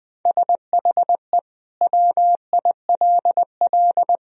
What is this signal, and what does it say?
Morse code, heard as a keyed tone, about 20 wpm, 700 Hz. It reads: SHE WILL
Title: Sentences from Top 1000 Words - Review Part 2 - 20wpm